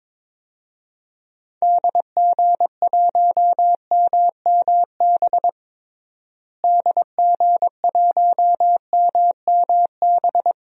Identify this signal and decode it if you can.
Morse code, heard as a keyed tone, about 22 wpm, 700 Hz. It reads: DG1MMB DG1MMB
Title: DXCC-100 Call Signs - 6 Characters - Repeated 2x - 22wpm